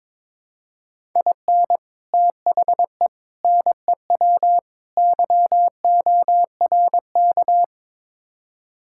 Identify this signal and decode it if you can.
Morse code, heard as a keyed tone, about 22 wpm, 700 Hz. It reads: IN THE NEW YORK
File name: practice / 22wpm / Build A Phrase - 5-Word Phrases - 22wpm